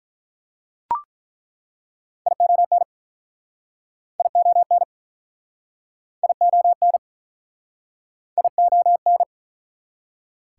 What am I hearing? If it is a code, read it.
Morse code, heard as a keyed tone, about 48 wpm, 700 Hz. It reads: ION ION ION ION